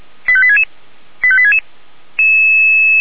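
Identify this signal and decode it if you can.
Five-tone selective call (ZVEI), heard as quick runs of five tones, tones of about 70 ms. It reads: 85866 85866
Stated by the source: ZVEI five-tone call received by SDR.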